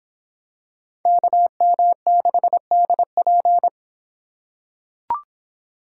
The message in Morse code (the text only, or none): KM6DP